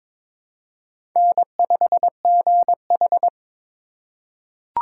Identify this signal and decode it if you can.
Morse code, heard as a keyed tone, about 22 wpm, 700 Hz. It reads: N5GH
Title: USA Call Signs - 4x Character Spacing - 22wpm